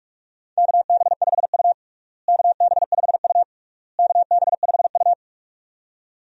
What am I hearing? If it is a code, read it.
Morse code, heard as a keyed tone, about 46 wpm, 700 Hz. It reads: KB5U KB5U KB5U
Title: USA Call Signs - 4 Characters - Repeated 3x - 45wpm